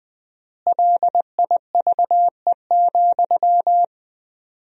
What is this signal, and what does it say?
Morse code, heard as a keyed tone, about 21 wpm, 700 Hz. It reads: LIVE,
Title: 4x Word-Spacing - Sentences from Top 200 Words - 20wpm